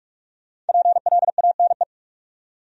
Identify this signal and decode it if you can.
Morse code, heard as a keyed tone, about 58 wpm, 690 Hz. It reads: PLANE